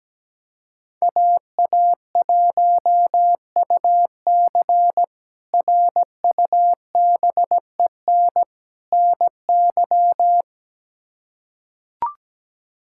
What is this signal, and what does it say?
Morse code, heard as a keyed tone, about 17 wpm, 700 Hz. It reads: AA1UC RUBEN NY
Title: K1USN SST QSO Practice - 17wpm